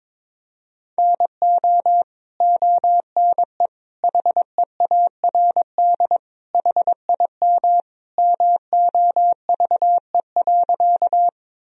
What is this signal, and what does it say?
Morse code, heard as a keyed tone, about 22 wpm, 700 Hz. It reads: NO ONE HEARD HIM MOVE.
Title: Sentences from Top 1000 Words - Review Part 1 - 22wpm